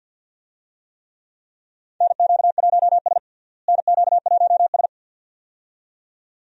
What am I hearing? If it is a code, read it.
Morse code, heard as a keyed tone, about 53 wpm, 700 Hz. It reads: NQ1S NQ1S
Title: USA Call Signs - 4 Characters - Repeated 2x - 50wpm